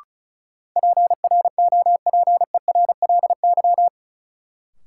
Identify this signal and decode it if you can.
Morse code, heard as a keyed tone, about 34 wpm, 700 Hz. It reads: PROPERLY
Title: Sentences from Top 1100 Words - New Word Drill Alt - 35wpm